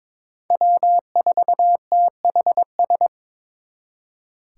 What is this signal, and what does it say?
Morse code, heard as a keyed tone, about 26 wpm, 700 Hz. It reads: W4THS
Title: Speed Racing - USA 1x3 Call Signs 22wpm